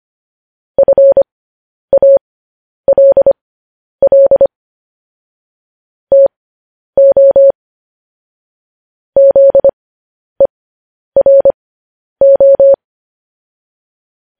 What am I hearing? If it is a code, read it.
Morse code, heard as a keyed tone, about 25 wpm, spaced as for 10 wpm, 550 Hz. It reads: FALL TO ZERO
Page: news-headlines-in-morse-code-at-10-wpm